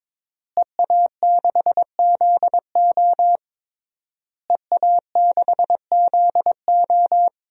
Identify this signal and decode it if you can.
Morse code, heard as a keyed tone, about 22 wpm, 700 Hz. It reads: EA6ZO EA6ZO
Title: DXCC-100 Call Signs - 5 Characters - Repeated 3x - 22wpm